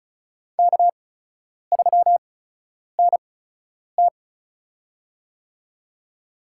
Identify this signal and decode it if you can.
Morse code, heard as a keyed tone, about 36 wpm, spaced as for 10 wpm, 700 Hz. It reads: K3NT
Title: USA Call Signs - 4 Characters - 8x Character Spacing - 35wpm